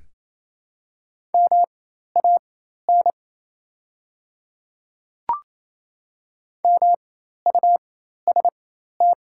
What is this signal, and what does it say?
Morse code, heard as a keyed tone, about 28 wpm, spaced as for 13 wpm, 700 Hz. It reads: MAN MUST
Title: 4x Character Spacing - Top 200 Words - 28wpm